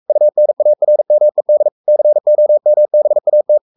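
Morse code, heard as a keyed tone, about 44 wpm, 600 Hz. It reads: UNARMED COMBAT
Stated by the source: Morse code at 43 wpm sending Ken Follett's novel Never in English, from CW med Gnister.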